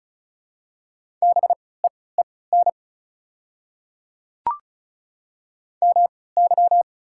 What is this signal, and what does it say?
Morse code, heard as a keyed tone, about 36 wpm, spaced as for 20 wpm, 690 Hz. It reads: BEEN MY
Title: 3x Character Spacing - Top 300 Words - 35wpm